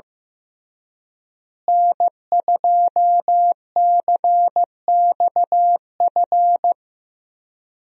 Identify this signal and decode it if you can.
Morse code, heard as a keyed tone, about 16 wpm, 700 Hz. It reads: N2CXF
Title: USA Call Signs - 5 Characters - Repeated 3x - 15wpm